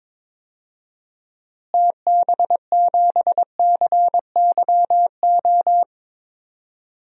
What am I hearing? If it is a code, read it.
Morse code, heard as a keyed tone, about 22 wpm, 700 Hz. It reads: TB7CYO
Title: DXCC-100 Call Signs - 6 Characters - Repeated 3x - 22wpm